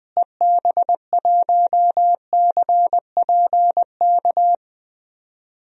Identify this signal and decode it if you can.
Morse code, heard as a keyed tone, about 20 wpm, 700 Hz. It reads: EB1CPK
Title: DXCC-100 Call Signs - 6 Characters - Repeated 3x - 20wpm